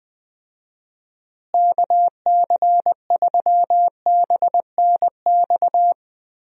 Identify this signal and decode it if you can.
Morse code, heard as a keyed tone, about 20 wpm, 700 Hz. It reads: KC3BNX
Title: USA Call Signs - 6 Characters - Repeated 2x - 20wpm